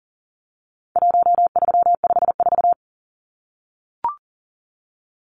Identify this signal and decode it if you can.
Morse code, heard as a keyed tone, about 40 wpm, 700 Hz. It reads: 1354
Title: Numbers - 4-Digit 40wpm